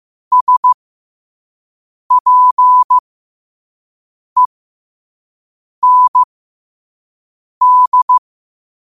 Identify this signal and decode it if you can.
Morse code, heard as a keyed tone, about 15 wpm, spaced as for 5 wpm, 1000 Hz. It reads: SPEND
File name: 5wpm